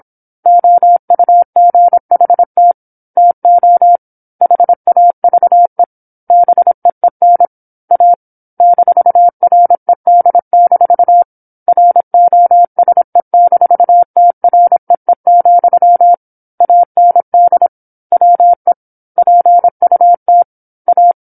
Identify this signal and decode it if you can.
Morse code, heard as a keyed tone, about 27 wpm, 700 Hz. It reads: OUGHT TO HAVE BEEN A -RED- ROSE-TREE, AND WE PUT A
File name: 08-Chapter-08 26 WPM